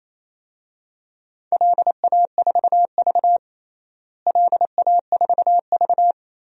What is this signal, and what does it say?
Morse code, heard as a keyed tone, about 28 wpm, 700 Hz. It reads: LA4V LA4V
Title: DXCC-100 Call Signs - 4 Characters - Repeated 2x - 28wpm